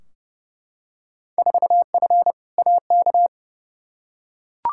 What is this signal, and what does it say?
Morse code, heard as a keyed tone, about 30 wpm, 700 Hz. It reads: 4F AK
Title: Field Day Exchanges Encore - 30wpm